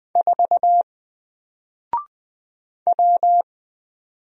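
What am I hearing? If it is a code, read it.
Morse code, heard as a keyed tone, about 20 wpm, 700 Hz. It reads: 4 W
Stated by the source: Single letter and number Morse drill at 20 wpm, from N4LCD.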